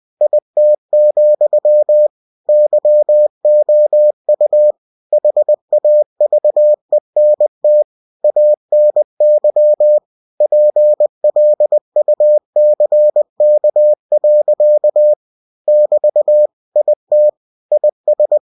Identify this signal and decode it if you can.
Morse code, heard as a keyed tone, about 20 wpm, 600 Hz. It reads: IT, YOU HAVENT ANY PLUCK. <BT> IT IS
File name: cw-books / winnie-the-pooh / winnie-20wpm-page-0062